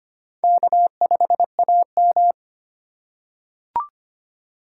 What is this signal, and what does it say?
Morse code, heard as a keyed tone, about 25 wpm, 700 Hz. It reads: K5AM
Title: K1USN SST QSO Practice - 25wpm